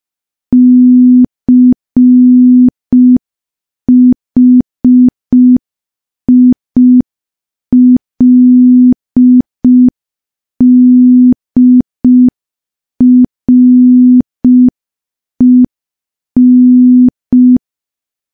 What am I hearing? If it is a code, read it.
Morse code, heard as a keyed tone, about 5 wpm, 250 Hz. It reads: CHILDREN
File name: morse / morse-training-05WPM